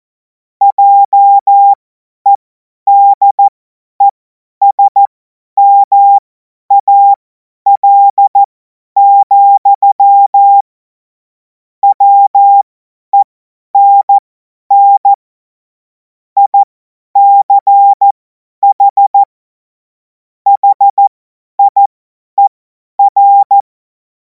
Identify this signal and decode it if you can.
Morse code, heard as a keyed tone, about 14 wpm, spaced as for 10 wpm, 800 Hz. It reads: JEDESMAL, WENN ICH HIER
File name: Morsekurs / Witze / Witz 17. März 2026 50 F70